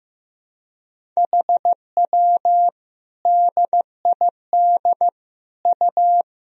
Morse code, heard as a keyed tone, about 16 wpm, 700 Hz. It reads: HW DID U
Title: Sentences from Top 100 Words and Abbreviations 15wpm